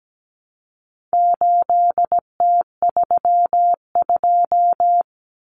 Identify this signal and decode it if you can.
Morse code, heard as a keyed tone, about 17 wpm, 700 Hz. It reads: 8T32
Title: Common Cut Numbers - 4-Digit 17wpm